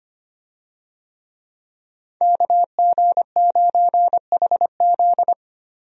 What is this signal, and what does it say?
Morse code, heard as a keyed tone, about 25 wpm, 700 Hz. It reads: KG9HZ
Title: USA Call Signs - 5 Characters - 25wpm